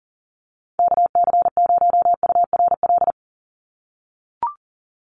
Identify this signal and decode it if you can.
Morse code, heard as a keyed tone, about 40 wpm, 700 Hz. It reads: KC0URL
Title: Call Signs - 40wpm